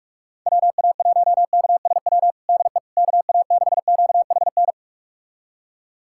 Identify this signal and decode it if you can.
Morse code, heard as a keyed tone, about 46 wpm, 700 Hz. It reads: WA1KSW DE KA6QHN
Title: USA Callsign DE Callsign - 6 Characters - 45wpm